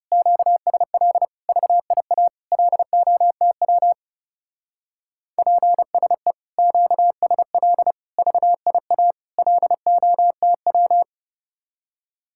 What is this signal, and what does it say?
Morse code, heard as a keyed tone, about 36 wpm, 700 Hz. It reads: QSL VIA LOTW PSE QSL VIA LOTW
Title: Speed-Racing Common QSO Phrases Encore - 30wpm